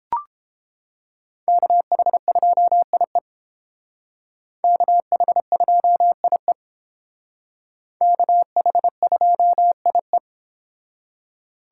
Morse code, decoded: KH2IE KH2IE KH2IE

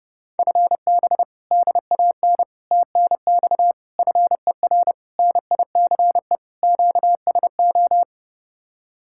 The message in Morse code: FB DAN TNX FER NICE QSO

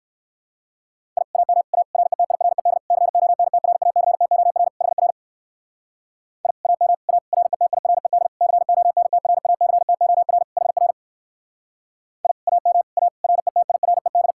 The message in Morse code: I AM A RETIRED CONTRACTOR HR I AM A RETIRED CONTRACTOR HR I AM A RETIRED

